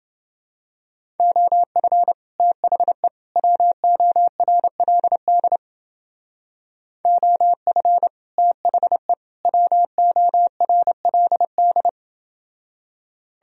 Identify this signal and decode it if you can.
Morse code, heard as a keyed tone, about 29 wpm, 700 Hz. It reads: OF THE WORLD OF THE WORLD